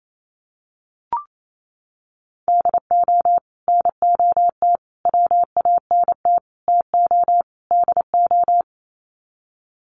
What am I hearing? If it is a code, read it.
Morse code, heard as a keyed tone, about 28 wpm, 700 Hz. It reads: DO NOT WANT TO DO
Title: Sets of 5 Words 28wpm